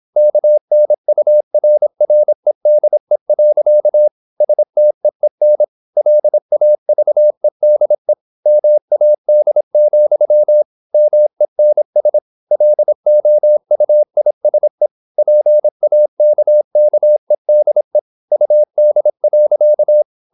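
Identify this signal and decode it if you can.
Morse code, heard as a keyed tone, about 26 wpm, 600 Hz. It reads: KNURREDE. STEEN LAVEDE MAD, MENS LOUISE PAKKEDE UD.
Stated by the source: High-speed (QRQ) Morse at 26 wpm.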